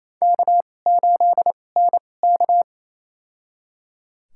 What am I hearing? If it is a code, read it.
Morse code, heard as a keyed tone, about 28 wpm, spaced as for 20 wpm, 700 Hz. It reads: K8NK
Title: USA Call Signs - 4 Characters - 2x Character Spacing - 28wpm